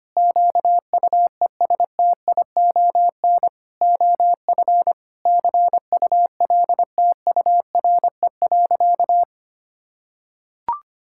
Morse code, heard as a keyed tone, about 25 wpm, 700 Hz. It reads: QUESTION OF CULTURE.
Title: Sentences from Top 1000 Words - Review Part 6 - 25wpm